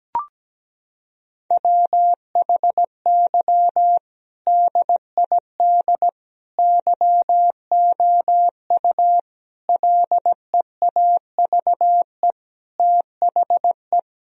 Morse code, heard as a keyed tone, about 17 wpm, 700 Hz. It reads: WHY DID YOU LEAVE THE
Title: Sentences from Top 1000 Words - Review Part 3 - 17wpm